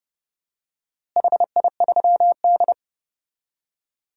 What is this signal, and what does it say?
Morse code, heard as a keyed tone, about 30 wpm, 700 Hz. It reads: HI3D
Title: DXCC-100 Call Signs - 4 Characters - Repeated 2x - 30wpm